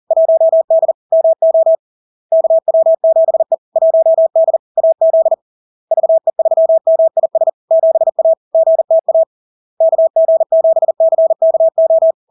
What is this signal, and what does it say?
Morse code, heard as a keyed tone, about 40 wpm, 650 Hz. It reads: 1D MO KW7E 1D AZ VE3MIS 7A GTA KG7CKO
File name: LARK-FD-2022-Log-in-CW-40wpm-650-Hz